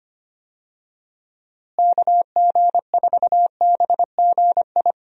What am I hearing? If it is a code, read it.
Morse code, heard as a keyed tone, about 25 wpm, 700 Hz. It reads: KG4BGI
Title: USA Call Signs - 6 Characters - Repeated 2x - 25wpm